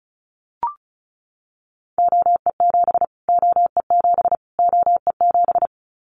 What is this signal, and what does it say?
Morse code, heard as a keyed tone, about 34 wpm, 700 Hz. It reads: OE7 OE7 OE7
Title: Callsign Prefixes for Top 25 DXCC - Repeated 3x - 35wpm